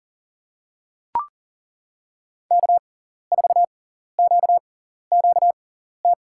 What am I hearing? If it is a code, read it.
Morse code, heard as a keyed tone, about 39 wpm, spaced as for 14 wpm, 700 Hz. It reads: K4QQT
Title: USA Call Signs - 5 Characters - 6x Character Spacing - 40wpm